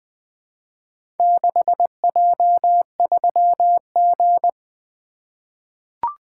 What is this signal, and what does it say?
Morse code, heard as a keyed tone, about 20 wpm, 700 Hz. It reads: 6J3G
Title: Four Letter-Number Combinations - 20wpm